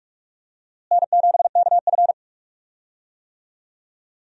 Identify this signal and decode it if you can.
Morse code, heard as a keyed tone, about 44 wpm, 700 Hz. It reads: N7KF
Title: USA Call Signs - 4 Characters - Repeated 3x - 45wpm